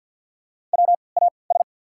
Morse code, heard as a keyed tone, about 54 wpm, spaced as for 29 wpm, 690 Hz. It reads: WAS
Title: 3x Character Spacing - Top 100 Words - 50wpm